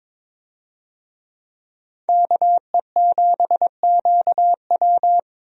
Morse code, heard as a keyed tone, about 22 wpm, 700 Hz. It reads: KE7QW
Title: USA Call Signs - 5 Characters - Repeated 2x - 22wpm